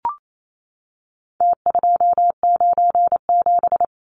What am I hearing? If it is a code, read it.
Morse code, heard as a keyed tone, about 28 wpm, 700 Hz. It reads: T297
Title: Numbers - Contest Style - 4-Digit 28wpm